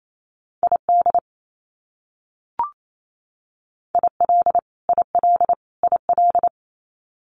Morse code, heard as a keyed tone, about 28 wpm, 700 Hz. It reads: ID IL IL IL
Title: US States Repeated 3x 28wpm